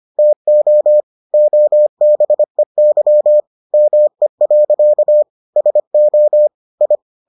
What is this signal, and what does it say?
Morse code, heard as a keyed tone, about 25 wpm, 600 Hz. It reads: TO OBEY ME. SO I